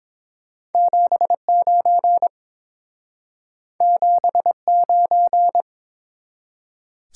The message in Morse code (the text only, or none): 79 79